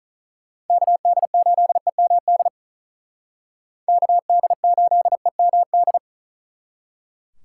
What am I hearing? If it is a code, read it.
Morse code, heard as a keyed tone, about 44 wpm, 700 Hz. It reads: KD8EMD KD8EMD
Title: Speed Racing - USA Call Signs 35wpm